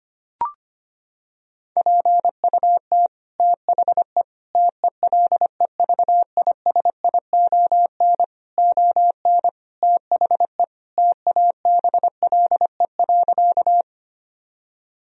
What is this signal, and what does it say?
Morse code, heard as a keyed tone, about 25 wpm, 700 Hz. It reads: PUT THE TELEVISION ON THE TABLE.
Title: Sentences from Top 1000 Words - Review Part 4 - 25wpm